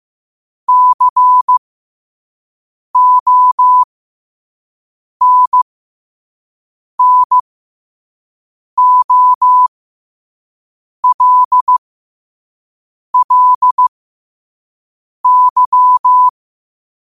Morse code, heard as a keyed tone, about 15 wpm, spaced as for 5 wpm, 1000 Hz. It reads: CONNOLLY